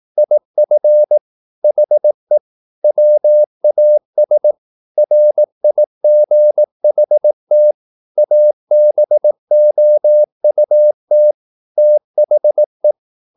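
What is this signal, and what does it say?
Morse code, heard as a keyed tone, about 18 wpm, 600 Hz. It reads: IF HE WAS RIGHT ABOUT THE
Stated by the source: Morse practice at 18 wpm, English text, from CW med Gnister.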